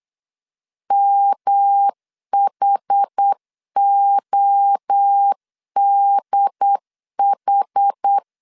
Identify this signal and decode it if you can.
Morse code, heard as a keyed tone, about 8 wpm, 800 Hz. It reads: MHODH